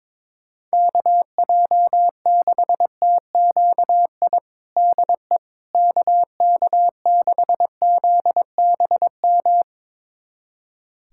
Morse code, heard as a keyed tone, about 22 wpm, 700 Hz. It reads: KJ6TQI DE KK6ZBM
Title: USA Callsign DE Callsign 22wpm